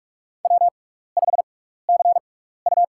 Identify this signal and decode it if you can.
Morse code, heard as a keyed tone, about 42 wpm, spaced as for 15 wpm, 700 Hz. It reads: W5CU
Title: USA Call Signs - 4 Characters - 6x Character Spacing - 45wpm